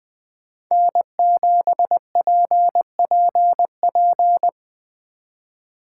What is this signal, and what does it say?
Morse code, heard as a keyed tone, about 24 wpm, 700 Hz. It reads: N7PPP
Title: Speed Racing - USA 1x3 Call Signs 20wpm